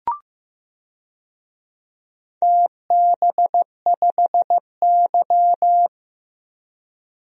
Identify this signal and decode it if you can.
Morse code, heard as a keyed tone, about 15 wpm, 700 Hz. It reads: TB5Y
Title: DXCC-100 Call Signs - 4 Characters - 15wpm